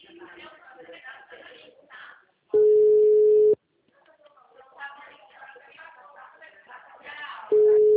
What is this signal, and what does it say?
Ringback tone (Europe, 425 Hz), with speech.